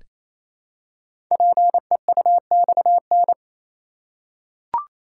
Morse code, heard as a keyed tone, about 27 wpm, 700 Hz. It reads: PEUXN
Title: Code Groups - 5 Letters - 28wpm